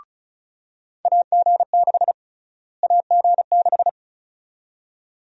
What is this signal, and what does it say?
Morse code, heard as a keyed tone, about 36 wpm, 700 Hz. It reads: AG6 AG6
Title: USA Call Sign Prefixes - Repeated 2x - 35wpm